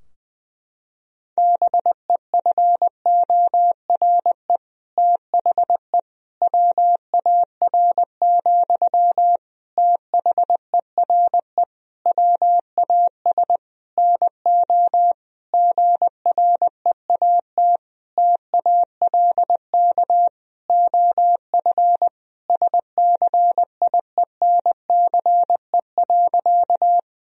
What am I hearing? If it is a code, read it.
Morse code, heard as a keyed tone, about 20 wpm, 700 Hz. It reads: BEFORE THE WAR, THERE WAS NO GREAT TALK OF SCIENCE.
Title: Sentences from Top 1000 Words - Review Part 2 - 20wpm